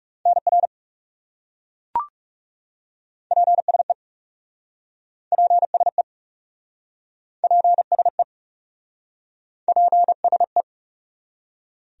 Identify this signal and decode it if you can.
Morse code, heard as a keyed tone, about 36 wpm, 700 Hz. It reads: NR PSE PSE PSE PSE
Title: Speed-Racing Abbreviations - 30wpm